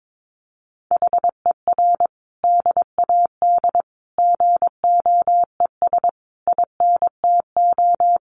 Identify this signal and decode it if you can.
Morse code, heard as a keyed tone, about 22 wpm, 700 Hz. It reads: HER DAD GOES INTO
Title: Additional Easy Phrases 22wpm